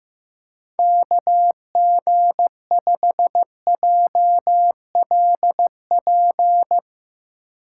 Morse code, heard as a keyed tone, about 15 wpm, 700 Hz. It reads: KG5JLP
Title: USA Call Signs - 3x Character Spacing - 15wpm